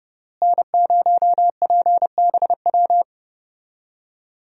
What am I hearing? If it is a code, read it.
Morse code, heard as a keyed tone, about 30 wpm, 700 Hz. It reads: N0PBW
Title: USA Call Signs - 5 Characters - Repeated 2x - 30wpm